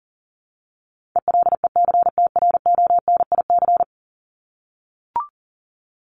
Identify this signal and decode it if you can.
Morse code, heard as a keyed tone, about 40 wpm, 700 Hz. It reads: ELECTRONIC
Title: Top 1001-1250 Words 40wpm